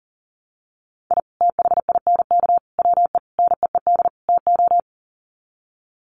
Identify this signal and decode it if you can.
Morse code, heard as a keyed tone, about 40 wpm, 700 Hz. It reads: I THINK WE NEED TO
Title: Sets of 5 Words 40wpm